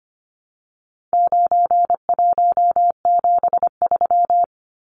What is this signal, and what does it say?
Morse code, heard as a keyed tone, about 25 wpm, 700 Hz. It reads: 9173
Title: Numbers - 4-Digit 25wpm